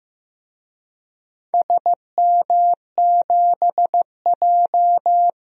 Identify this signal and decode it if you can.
Morse code, heard as a keyed tone, about 15 wpm, 700 Hz. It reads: SM7J
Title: DXCC-100 Call Signs - 4 Characters - Repeated 2x - 15wpm